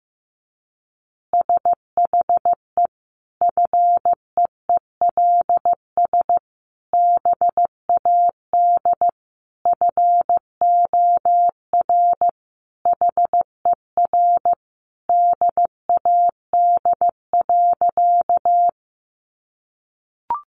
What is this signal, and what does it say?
Morse code, heard as a keyed tone, about 15 wpm, 700 Hz. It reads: SHE FEELS BAD FOR HER DAD.
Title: Additional Easy Phrases 15wpm